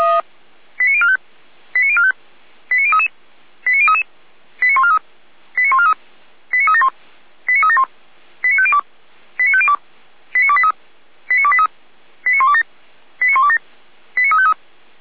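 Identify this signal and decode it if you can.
Five-tone selective call (ZVEI), heard as quick runs of five tones, tones of about 80 ms. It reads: 89045 89045 89033 89033 89143 89143 89371 89371 89502 89502 89283 89283 89127 89127 89353
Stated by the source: ZVEI five-tone selective call received by SDR.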